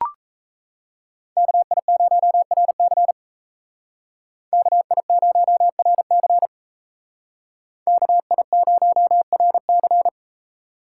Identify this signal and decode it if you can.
Morse code, heard as a keyed tone, about 36 wpm, 700 Hz. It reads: KI0RC KI0RC KI0RC